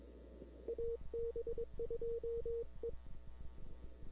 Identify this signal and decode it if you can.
Morse code, heard as a keyed tone, about 22 wpm, 460 Hz. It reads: AB2E
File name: CW / 2005-k5tr arrl cw dx -00:00